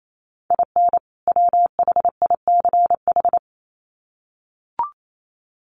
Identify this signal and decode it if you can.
Morse code, heard as a keyed tone, about 28 wpm, 700 Hz. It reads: IN WHICH
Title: Sets of 2 Words 28wpm